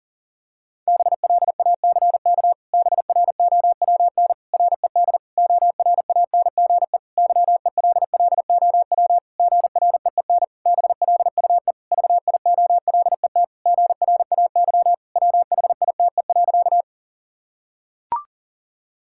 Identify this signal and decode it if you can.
Morse code, heard as a keyed tone, about 40 wpm, 700 Hz. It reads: BLACK BROWN RED ORANGE YELLOW GREEN BLUE VIOLET GRAY WHITE.
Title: Pangram Challenge v2 - 40wpm